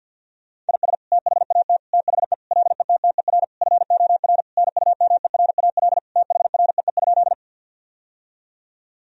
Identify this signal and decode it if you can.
Morse code, heard as a keyed tone, about 51 wpm, 700 Hz. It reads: IS THAT THE LETTER FOR NUMERAL THREE?
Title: Sentences from Top 1000 Words - Review Part 2 - 50wpm